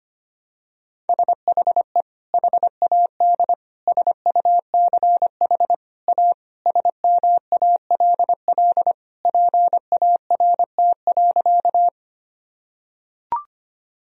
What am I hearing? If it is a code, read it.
Morse code, heard as a keyed tone, about 25 wpm, 700 Hz. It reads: SHE HAD SUCH A SMALL PART.
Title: Sentences from Top 1000 Words - Review Part 1 - 25wpm